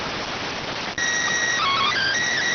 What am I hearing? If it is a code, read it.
SSTV (slow-scan television), PD 290.